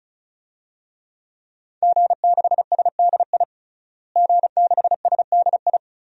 Morse code, heard as a keyed tone, about 34 wpm, 700 Hz. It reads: G6SDI G6SDI
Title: DXCC-100 Call Signs - 5 Characters - Repeated 2x - 35wpm